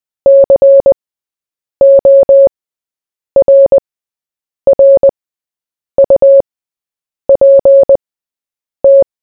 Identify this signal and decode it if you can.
Morse code, heard as a keyed tone, about 20 wpm, spaced as for 8 wpm, 550 Hz. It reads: CORRUPT
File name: Short-Story-4-8wpm